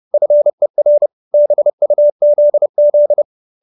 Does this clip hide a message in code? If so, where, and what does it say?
Yes: Morse code from 0.1 to 3.2 s, heard as a keyed tone, about 30 wpm, 590 Hz. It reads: FER BUZZ